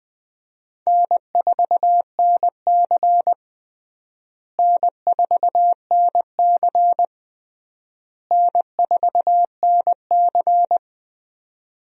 Morse code, heard as a keyed tone, about 20 wpm, 700 Hz. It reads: N4NC N4NC N4NC
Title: USA Call Signs - 4 Characters - Repeated 3x - 20wpm